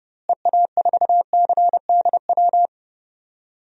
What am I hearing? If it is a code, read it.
Morse code, heard as a keyed tone, about 29 wpm, 700 Hz. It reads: EA4CDW
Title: DXCC-100 Call Signs - 6 Characters - Repeated 3x - 30wpm